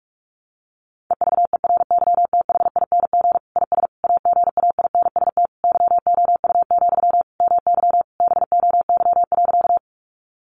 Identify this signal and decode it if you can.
Morse code, heard as a keyed tone, about 44 wpm, 700 Hz. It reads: EVERYTHING IS AGAINST YOU, MY BOY.
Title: Sentences from Top 700 Words 45wpm